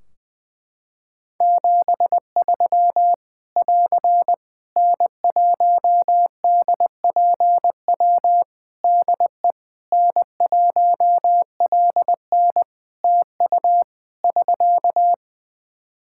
Morse code, heard as a keyed tone, about 21 wpm, 700 Hz. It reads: 73 <AR> N1DPW DE N1LN TU <SK>
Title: Practice QSO - Exchange 4 - Ending - 20wpm